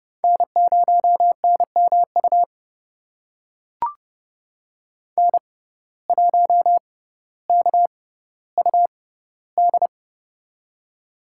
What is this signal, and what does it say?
Morse code, heard as a keyed tone, about 30 wpm, spaced as for 15 wpm, 700 Hz. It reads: N0NMU N1KUD
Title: USA Call Signs - 5 Characters - 6x Character Spacing - 30wpm